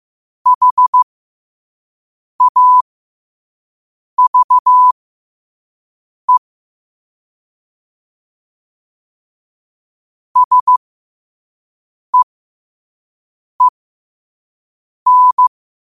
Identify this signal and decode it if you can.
Morse code, heard as a keyed tone, about 15 wpm, spaced as for 5 wpm, 1000 Hz. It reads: HAVE SEEN
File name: qotdfiles / 5wpm